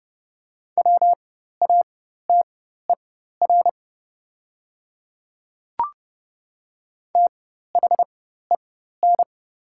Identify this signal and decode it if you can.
Morse code, heard as a keyed tone, about 30 wpm, spaced as for 14 wpm, 700 Hz. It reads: WATER THEN
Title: 4x Character Spacing - Top 100 Words - 30wpm